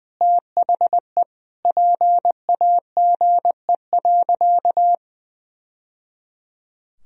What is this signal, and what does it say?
Morse code, heard as a keyed tone, about 20 wpm, 700 Hz. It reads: THE PAGE.
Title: Sentences from Top 1000 Words - Review Part 2 - 20wpm